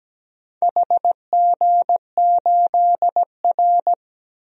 Morse code, heard as a keyed tone, about 17 wpm, 700 Hz. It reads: HG8R
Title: DXCC-100 Call Signs - 4 Characters - 17wpm